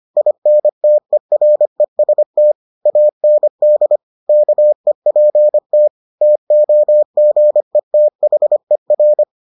Morse code, heard as a keyed tone, about 25 wpm, 600 Hz. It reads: INTEREST AND KEPT TOGETHER